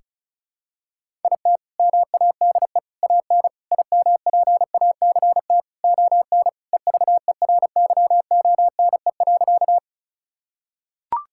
Morse code, heard as a keyed tone, about 36 wpm, 700 Hz. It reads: IT MADE AN IMPACT ON EVERYONE.
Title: Sentences from Top 1000 Words - Review Part 6 - 35wpm